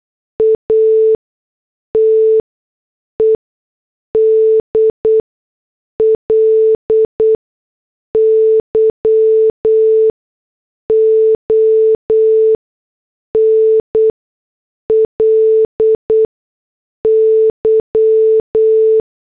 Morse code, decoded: ATEDLY ONLY